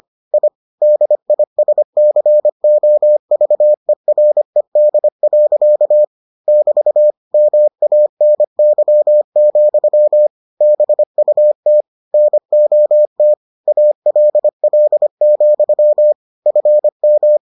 Morse code, heard as a keyed tone, about 25 wpm, 600 Hz. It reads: I DISCOVERED. <BT> MANY, BUT NOT ALL, FM